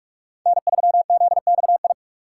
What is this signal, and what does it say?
Morse code, heard as a keyed tone, about 44 wpm, 700 Hz. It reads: N3ZXI
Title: USA Call Signs - 5 Characters - 6x Character Spacing - 45wpm